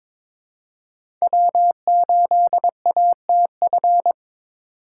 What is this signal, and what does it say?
Morse code, heard as a keyed tone, about 26 wpm, 700 Hz. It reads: W8ATF
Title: Speed Racing - USA Call Signs 22wpm